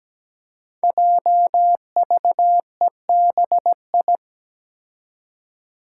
Morse code, heard as a keyed tone, about 17 wpm, 700 Hz. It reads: JVEBI
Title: Five Challenging Letter-Number Combinations 17wpm